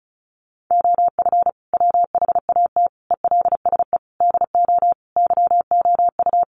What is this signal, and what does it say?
Morse code, heard as a keyed tone, about 34 wpm, 700 Hz. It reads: OF WHAT ELSE DO YOU
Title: Sentences from Top 1400 Words 35wpm